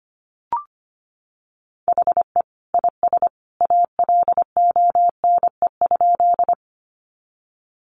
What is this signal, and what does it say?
Morse code, heard as a keyed tone, about 25 wpm, 700 Hz. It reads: HE IS ALONE?
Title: Sentences from Top 1000 Words 25wpm